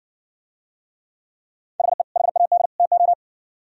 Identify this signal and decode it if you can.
Morse code, heard as a keyed tone, about 68 wpm, 700 Hz. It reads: HE HAD TO